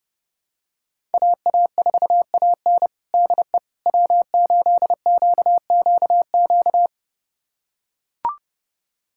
Morse code, heard as a keyed tone, about 30 wpm, 700 Hz. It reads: AA4AN DE W8QQQ
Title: USA Callsign DE Callsign 30wpm